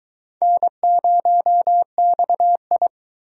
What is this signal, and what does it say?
Morse code, heard as a keyed tone, about 22 wpm, 700 Hz. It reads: N0XI